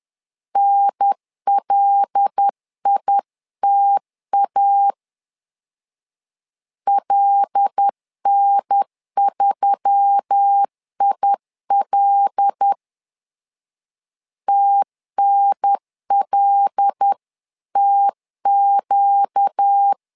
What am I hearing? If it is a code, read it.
Morse code, heard as a keyed tone, about 10 wpm, 800 Hz. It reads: NLITA LN3IL TNLTQ
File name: LEZIONI / 5 LBVC / 5 L 10wpm 2259